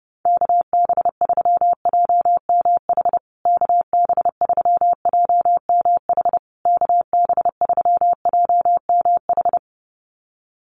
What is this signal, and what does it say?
Morse code, heard as a keyed tone, about 28 wpm, 700 Hz. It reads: KB3JMH KB3JMH KB3JMH
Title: USA Call Signs Repeated 3x 30wpm